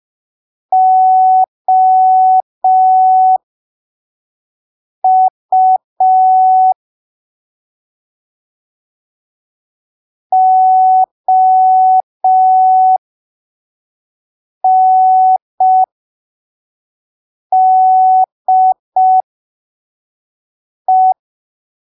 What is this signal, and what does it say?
Morse code, heard as a keyed tone, about 5 wpm, spaced as for 3 wpm, 750 Hz. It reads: OU ONDE
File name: arquivos / texto / texto 13 05ppm